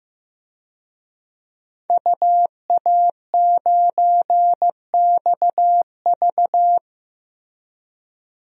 Morse code, heard as a keyed tone, about 15 wpm, 700 Hz. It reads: UA9XV
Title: DXCC-100 Call Signs - 5 Characters - Repeated 3x - 15wpm